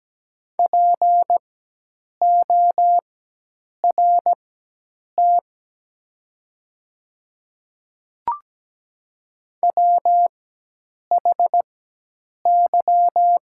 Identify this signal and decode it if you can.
Morse code, heard as a keyed tone, about 17 wpm, spaced as for 8 wpm, 700 Hz. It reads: PORT WHY
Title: 4x Character Spacing - Top 200 Words - 17wpm